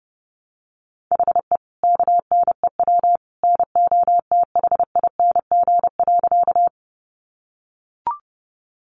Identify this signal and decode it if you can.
Morse code, heard as a keyed tone, about 30 wpm, 700 Hz. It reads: HE KNEW NOTHING.
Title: Sentences from Top 500 Words 30wpm